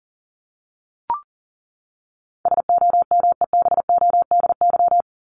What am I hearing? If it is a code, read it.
Morse code, heard as a keyed tone, about 40 wpm, 700 Hz. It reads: SOMEBODY
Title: Top 1751-2000 Words 40wpm